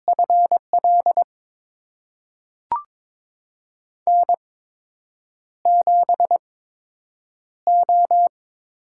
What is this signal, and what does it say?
Morse code, heard as a keyed tone, about 22 wpm, spaced as for 8 wpm, 700 Hz. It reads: FL N7O
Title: USA Call Signs - 5 Characters - 8x Character Spacing - 22wpm